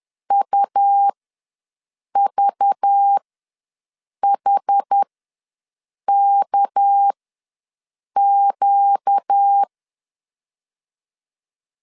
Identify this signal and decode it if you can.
Morse code, heard as a keyed tone, about 10 wpm, spaced as for 6 wpm, 800 Hz. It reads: UVHKQ
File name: LEZIONI / 7 GF9Z / 7 GF9 ws 10wpm 4034